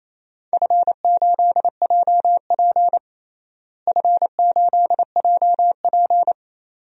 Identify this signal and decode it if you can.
Morse code, heard as a keyed tone, about 28 wpm, 700 Hz. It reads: F8JP F8JP
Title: DXCC-100 Call Signs - 4 Characters - Repeated 2x - 28wpm